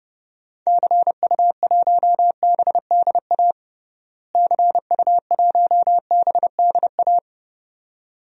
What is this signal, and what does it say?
Morse code, heard as a keyed tone, about 30 wpm, 700 Hz. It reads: CU1BDA CU1BDA
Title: DXCC-100 Call Signs - 6 Characters - Repeated 2x - 30wpm